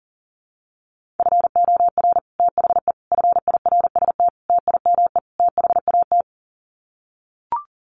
Morse code, heard as a keyed tone, about 40 wpm, 700 Hz. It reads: FOR THE FIRST TIME THAT
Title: Sets of 5 Words Encore 40wpm